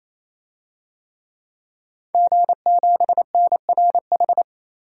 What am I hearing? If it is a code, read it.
Morse code, heard as a keyed tone, about 28 wpm, 700 Hz. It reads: G7NRH
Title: DXCC-100 Call Signs - 5 Characters - Repeated 3x - 28wpm